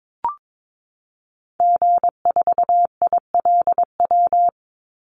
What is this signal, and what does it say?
Morse code, heard as a keyed tone, about 22 wpm, 690 Hz. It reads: G4ILW
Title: Call Signs Repeated 2x - 22wpm